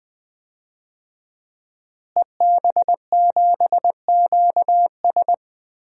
Morse code, heard as a keyed tone, about 20 wpm, 700 Hz. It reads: EB7QS
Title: DXCC-100 Call Signs - 5 Characters - Repeated 3x - 20wpm